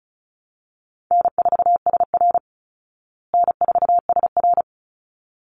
Morse code, heard as a keyed tone, about 34 wpm, 700 Hz. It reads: N4SR N4SR
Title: Easy Call Signs Repeated 2x - 35wpm